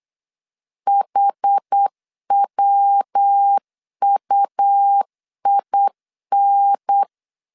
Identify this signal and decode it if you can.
Morse code, heard as a keyed tone, about 8 wpm, 800 Hz. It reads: HWUIN
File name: CW / DET / 1 3 1500C S B 8wpm 4141